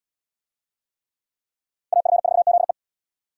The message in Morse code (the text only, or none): AF4ZE